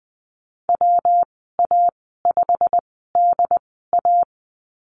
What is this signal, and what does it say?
Morse code, heard as a keyed tone, about 20 wpm, spaced as for 15 wpm, 700 Hz. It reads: WA5DA
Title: 154 Lesson 25g - Letter p - Call Signs 20wpm